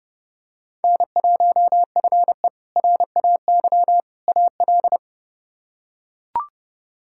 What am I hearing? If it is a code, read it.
Morse code, heard as a keyed tone, about 32 wpm, 700 Hz. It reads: N1FE RAY AL
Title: K1USN SST QSO Practice - 30wpm